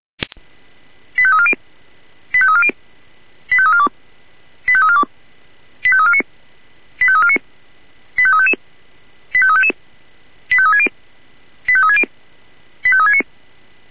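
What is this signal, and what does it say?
Five-tone selective call (ZVEI), heard as quick runs of five tones, tones of about 70 ms. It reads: 85340 85340 85342 85342 85349 85349 85355 85360 85360 85369
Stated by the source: ZVEI five-tone selective call received by SDR.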